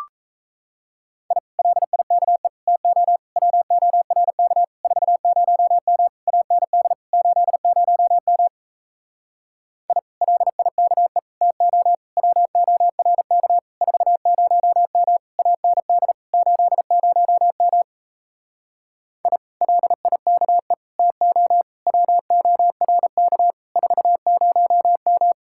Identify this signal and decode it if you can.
Morse code, heard as a keyed tone, about 38 wpm, 700 Hz. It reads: I LIKE TO WORK 40M AND 80M I LIKE TO WORK 40M AND 80M I LIKE TO WORK 40M